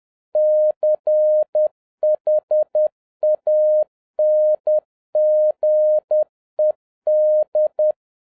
Morse code, heard as a keyed tone, about 10 wpm, 610 Hz. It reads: CHANGED